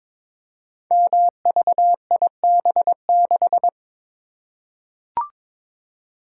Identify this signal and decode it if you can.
Morse code, heard as a keyed tone, about 22 wpm, 700 Hz. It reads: MVIB6
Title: Code Groups - 5 Letter:Numbers - 22wpm